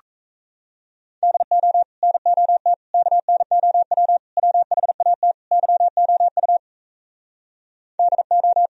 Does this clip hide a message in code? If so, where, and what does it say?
Yes: Morse code from 1.2 to 8.7 s, heard as a keyed tone, about 38 wpm, 700 Hz. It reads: DO NOT KNOW WHAT YOU DO